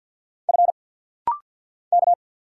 Morse code, heard as a keyed tone, about 50 wpm, 700 Hz. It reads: F K
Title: Single Letter-Number Alt - Rapid-Fire 50wpm